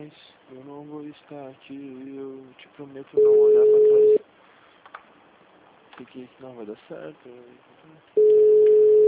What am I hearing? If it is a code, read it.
Ringback tone (Europe, 425 Hz), with speech.
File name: out- -2010- - - .21004